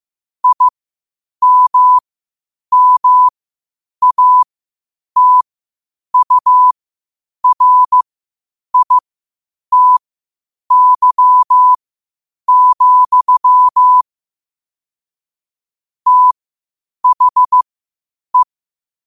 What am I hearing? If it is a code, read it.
Morse code, heard as a keyed tone, about 15 wpm, spaced as for 8 wpm, 1000 Hz. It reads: IMMATURITY, THE
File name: qotdfiles / 8wpm